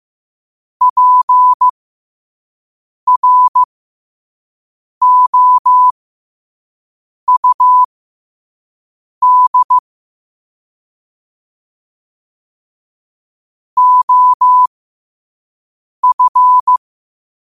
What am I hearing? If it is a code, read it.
Morse code, heard as a keyed tone, about 15 wpm, spaced as for 5 wpm, 1000 Hz. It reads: PROUD OF